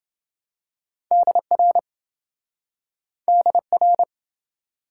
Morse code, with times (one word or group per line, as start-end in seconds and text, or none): 1.1-1.8 DR
3.3-4.0 DR